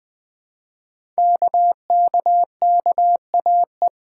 Morse code, heard as a keyed tone, about 20 wpm, 700 Hz. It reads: KKKAE
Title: Code Groups - 5 Letters - 20wpm